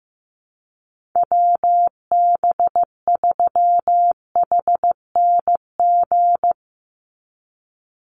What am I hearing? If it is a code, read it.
Morse code, heard as a keyed tone, about 15 wpm, 700 Hz. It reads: WB3HNG
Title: USA Call Signs 15wpm